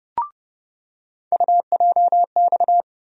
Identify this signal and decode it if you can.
Morse code, heard as a keyed tone, about 29 wpm, 700 Hz. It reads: UJX
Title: USA Call Sign Postfixes - 3 Letters 30wpm